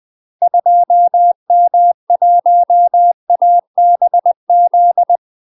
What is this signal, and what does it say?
Morse code, heard as a keyed tone, about 20 wpm, 700 Hz. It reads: 2M1ABZ
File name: UK MASTER QSOs TRAINING / 20WPM / UK MASTER QSOs 0004